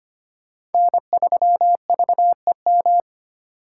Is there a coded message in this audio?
Yes: Morse code from 0.7 to 3.0 s, heard as a keyed tone, about 32 wpm, 700 Hz. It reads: N3VEM